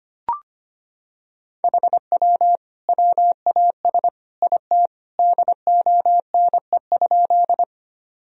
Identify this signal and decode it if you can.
Morse code, heard as a keyed tone, about 25 wpm, 700 Hz. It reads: HW WAS IT DONE?
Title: Sentences from Top 500 Words and Abbreviations 25wpm